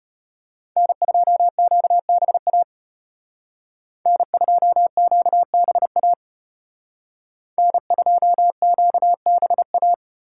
Morse code, heard as a keyed tone, about 30 wpm, 700 Hz. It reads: N2QBA N2QBA N2QBA
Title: Speed Racing - USA Call Signs 25wpm